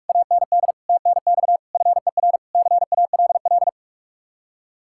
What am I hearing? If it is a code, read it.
Morse code, heard as a keyed tone, about 67 wpm, spaced as for 56 wpm, 700 Hz. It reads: AND TNX FER CALL
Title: Speed-Racing Common QSO Phrases - 45wpm